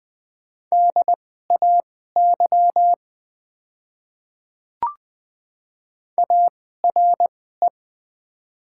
Morse code, heard as a keyed tone, about 20 wpm, spaced as for 14 wpm, 700 Hz. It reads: DAY ARE
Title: 2x Character Spacing - Top 300 Words - 20wpm